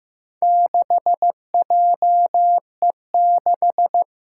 Morse code, heard as a keyed tone, about 15 wpm, 700 Hz. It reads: 6JE6
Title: Four Challenging Letter-Number Combinations 15wpm